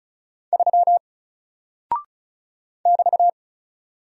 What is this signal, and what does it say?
Morse code, heard as a keyed tone, about 36 wpm, 700 Hz. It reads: 3 <BT>
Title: Single Character (Letter:Number:Punctuation:Prosign) Alt 35wpm